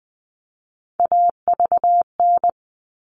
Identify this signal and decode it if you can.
Morse code, heard as a keyed tone, about 20 wpm, 700 Hz. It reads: AVN
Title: All Cut Numbers - 3-Digit 20wpm